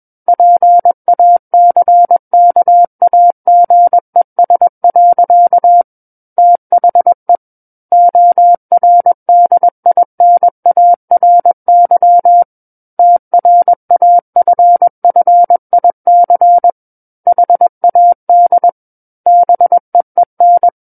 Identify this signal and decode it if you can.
Morse code, heard as a keyed tone, about 21 wpm, 700 Hz. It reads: PACKAGES. THE ORDINARY TRAFFIC HAD BEEN
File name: Book One / 12-Chapter 12 - 21 WPM